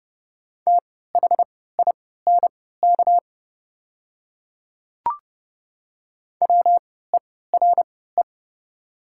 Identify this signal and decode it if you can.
Morse code, heard as a keyed tone, about 29 wpm, spaced as for 17 wpm, 690 Hz. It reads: THINK WERE
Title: 3x Character Spacing - Top 200 Words - 30wpm